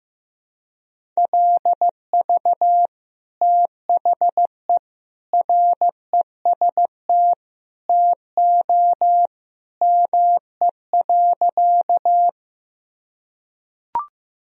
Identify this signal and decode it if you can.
Morse code, heard as a keyed tone, about 15 wpm, 700 Hz. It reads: LV THE REST TO ME.
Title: Sentences from Top 500 Words and Abbreviations 15wpm